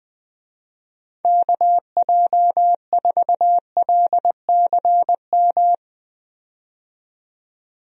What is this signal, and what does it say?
Morse code, heard as a keyed tone, about 20 wpm, 700 Hz. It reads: KJ4LCM
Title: USA Call Signs - 6 Characters - 20wpm